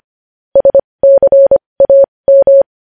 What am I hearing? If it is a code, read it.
Morse code, heard as a keyed tone, about 25 wpm, spaced as for 20 wpm, 550 Hz. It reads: SCAM
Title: News Headlines at 20